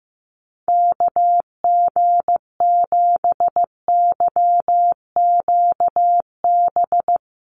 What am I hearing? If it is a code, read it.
Morse code, heard as a keyed tone, about 15 wpm, 700 Hz. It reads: KG7YQB
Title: USA Call Signs Repeated 2x 15wpm